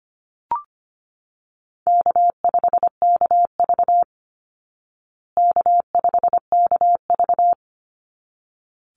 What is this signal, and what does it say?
Morse code, heard as a keyed tone, about 25 wpm, 700 Hz. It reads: K5KV K5KV
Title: Call Signs Repeated 3x - 25wpm